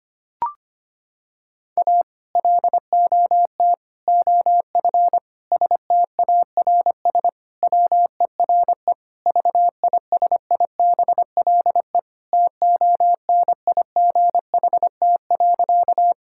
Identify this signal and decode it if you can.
Morse code, heard as a keyed tone, about 25 wpm, 700 Hz. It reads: A LOT OF STARS WERE VISIBLE TONIGHT.
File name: practice / 25wpm / Sentences with 2-Word Phrases - 25wpm